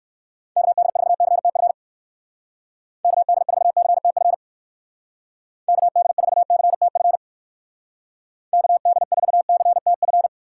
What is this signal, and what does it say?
Morse code, heard as a keyed tone, about 59 wpm, 700 Hz. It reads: KD4CTF KD4CTF KD4CTF KD4CTF